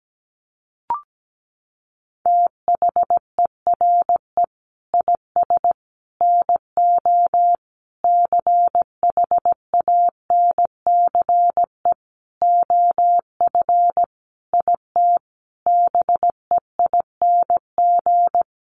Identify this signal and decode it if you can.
Morse code, heard as a keyed tone, about 17 wpm, 700 Hz. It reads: THERE IS NO CHANCE OF IT BEING
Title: Sentences from Top 800 Words 17wpm